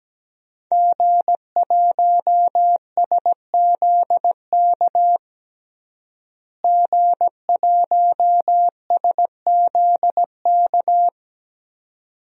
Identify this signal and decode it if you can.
Morse code, heard as a keyed tone, about 17 wpm, 700 Hz. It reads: G1SZK G1SZK
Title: DXCC-100 Call Signs - 5 Characters - Repeated 2x - 17wpm